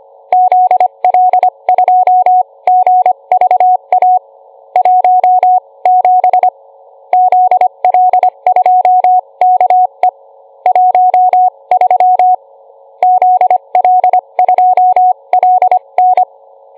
Morse code, heard as a keyed tone, about 25 wpm, spaced as for 22 wpm, 730 Hz. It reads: ZL2GVA 17 ZL2KE 13 ZL2LN